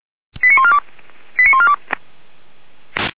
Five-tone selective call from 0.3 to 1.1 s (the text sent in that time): 89142